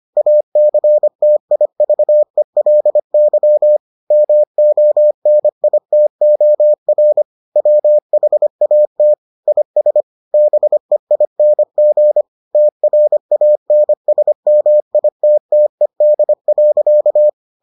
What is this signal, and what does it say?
Morse code, heard as a keyed tone, about 25 wpm, 600 Hz. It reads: ACTIVELY MONITOR WHAT IS BEING TRANSMITTED.